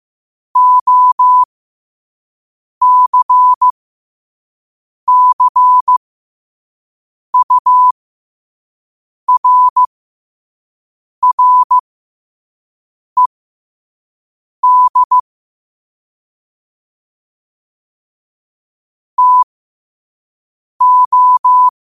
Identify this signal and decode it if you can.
Morse code, heard as a keyed tone, about 15 wpm, spaced as for 5 wpm, 1000 Hz. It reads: OCCURRED TO